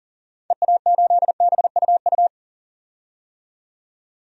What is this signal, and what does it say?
Morse code, heard as a keyed tone, about 40 wpm, 700 Hz. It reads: EA8BUU
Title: DXCC-100 Call Signs - 6 Characters - 40wpm